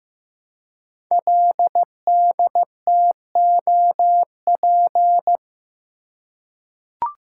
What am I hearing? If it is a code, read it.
Morse code, heard as a keyed tone, about 15 wpm, 700 Hz. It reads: LDTOP
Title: Code Groups - 5 Letters - 15wpm